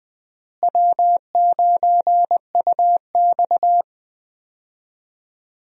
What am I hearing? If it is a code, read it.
Morse code, heard as a keyed tone, about 20 wpm, 700 Hz. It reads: W9UX
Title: USA Call Signs - 4 Characters - Repeated 3x - 20wpm